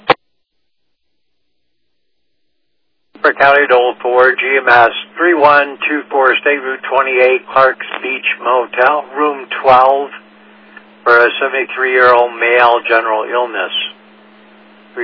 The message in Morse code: none

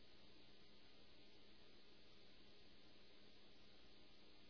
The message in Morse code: none

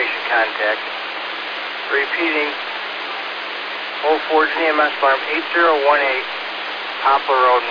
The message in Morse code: none

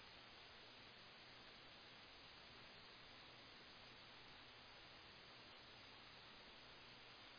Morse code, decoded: none

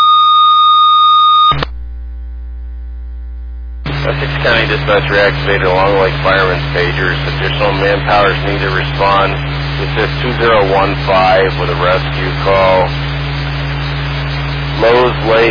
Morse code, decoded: none

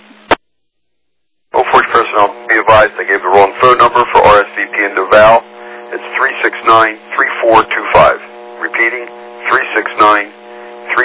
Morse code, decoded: none

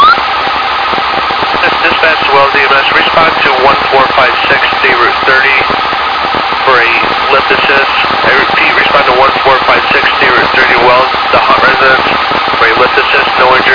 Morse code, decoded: none

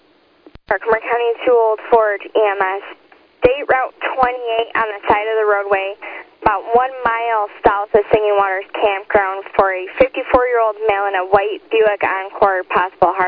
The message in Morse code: none

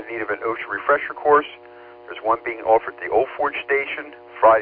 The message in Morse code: none

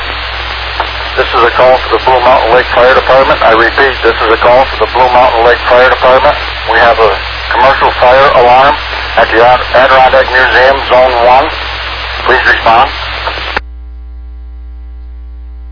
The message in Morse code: none

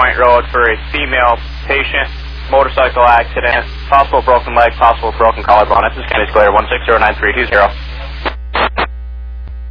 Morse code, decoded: none

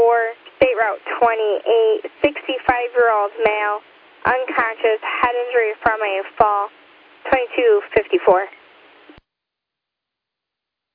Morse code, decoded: none

